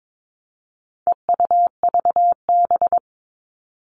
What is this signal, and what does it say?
Morse code, heard as a keyed tone, about 22 wpm, 700 Hz. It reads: EUVB
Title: All Cut Numbers - 4-Digit 22wpm